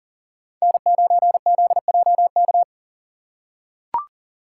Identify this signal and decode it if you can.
Morse code, heard as a keyed tone, about 40 wpm, 700 Hz. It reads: N9ZJK
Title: USA Call Signs - 5 Characters - 4x Character Spacing - 40wpm